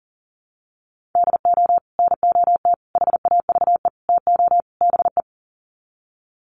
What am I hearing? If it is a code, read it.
Morse code, heard as a keyed tone, about 40 wpm, 700 Hz. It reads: DO NOT HAVE TO BE
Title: Sets of 5 Words 40wpm